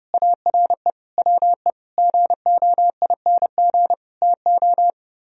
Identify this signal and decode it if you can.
Morse code, heard as a keyed tone, about 30 wpm, 700 Hz. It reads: ARE WE GOING TO